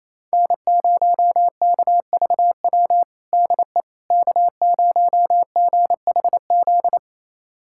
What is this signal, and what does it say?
Morse code, heard as a keyed tone, about 28 wpm, 700 Hz. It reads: N0KVW DE K0GHZ
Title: USA Callsign DE Callsign - 5 Characters - 28wpm